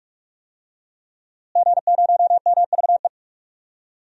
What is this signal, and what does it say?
Morse code, heard as a keyed tone, about 44 wpm, 700 Hz. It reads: G0MVE
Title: DXCC-100 Call Signs - 5 Characters - Repeated 3x - 45wpm